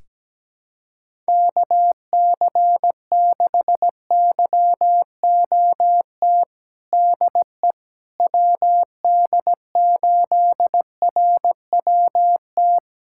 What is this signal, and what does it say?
Morse code, heard as a keyed tone, about 17 wpm, 700 Hz. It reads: KC6YOT DE WD8RWT
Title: USA Callsign DE Callsign 17wpm